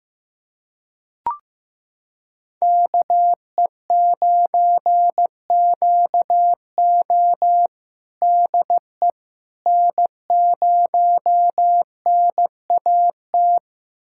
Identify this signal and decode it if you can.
Morse code, heard as a keyed tone, about 16 wpm, 700 Hz. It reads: KE9QO DE N0NAT
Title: USA Callsign DE Callsign - 5 Characters - 15wpm